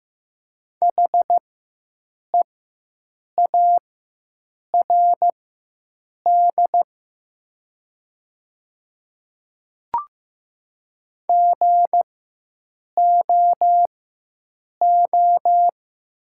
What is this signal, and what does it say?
Morse code, heard as a keyed tone, about 15 wpm, spaced as for 7 wpm, 700 Hz. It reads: HEARD GOO
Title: 4x Character Spacing - Top 500 Words - 15wpm